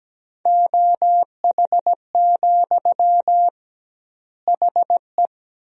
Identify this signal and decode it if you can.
Morse code, heard as a keyed tone, about 17 wpm, 700 Hz. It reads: OH, HE
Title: 2x Word-Spacing - Sentences from Top 500 Words - 17wpm